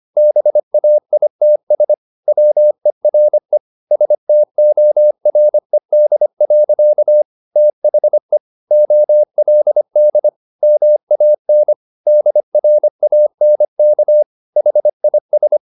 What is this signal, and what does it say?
Morse code, heard as a keyed tone, about 25 wpm, 600 Hz. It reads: BAITS WERE STORED. THE OLD MAN DRANK HIS
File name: the old man and the sea part 1 25wpm